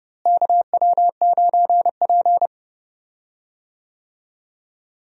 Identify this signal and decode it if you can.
Morse code, heard as a keyed tone, about 30 wpm, 700 Hz. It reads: KW9P
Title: USA Call Signs - 4 Characters - Repeated 3x - 30wpm